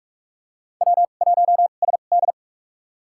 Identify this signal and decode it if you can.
Morse code, heard as a keyed tone, about 46 wpm, spaced as for 32 wpm, 700 Hz. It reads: W1SD
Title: USA Call Signs - 2x Character Spacing - 45wpm